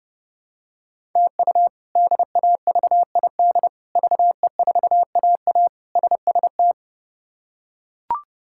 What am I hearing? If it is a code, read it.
Morse code, heard as a keyed tone, about 32 wpm, 700 Hz. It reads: TU DAVID VE4AA SST
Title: K1USN SST QSO Practice - 30wpm